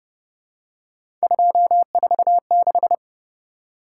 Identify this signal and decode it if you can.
Morse code, heard as a keyed tone, about 33 wpm, 700 Hz. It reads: 246